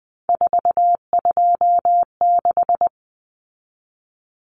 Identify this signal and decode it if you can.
Morse code, heard as a keyed tone, about 20 wpm, 700 Hz. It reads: 426
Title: Common Cut Numbers - 3-Digit 20wpm